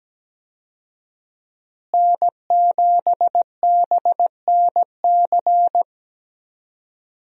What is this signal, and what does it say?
Morse code, heard as a keyed tone, about 17 wpm, 700 Hz. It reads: N7BNC
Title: USA Call Signs - 5 Characters - Repeated 2x - 17wpm